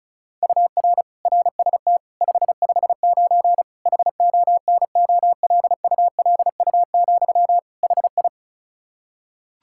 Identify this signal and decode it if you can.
Morse code, heard as a keyed tone, about 36 wpm, 700 Hz. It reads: UR RST 559 HONOLULU, HI
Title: Practice QSO - Exchange 1 - Beginning - 35wpm